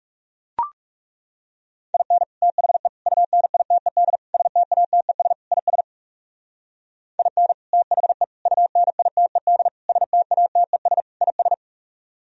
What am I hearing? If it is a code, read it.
Morse code, heard as a keyed tone, about 36 wpm, 700 Hz. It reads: IN THE UNITED STATES IS IN THE UNITED STATES IS
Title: Speed-Racing - Sets of 5 Words 30wpm